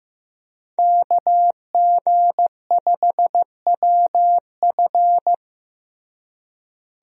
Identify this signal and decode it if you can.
Morse code, heard as a keyed tone, about 15 wpm, 700 Hz. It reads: KG5WF
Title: USA Call Signs - 5 Characters - Repeated 3x - 15wpm